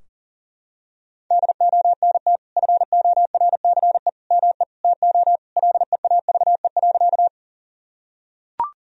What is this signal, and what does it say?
Morse code, heard as a keyed tone, about 40 wpm, 700 Hz. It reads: DONT FORCE ME TO LEAVE.
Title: Sentences from Top 1000 Words - Review Part 3 - 40wpm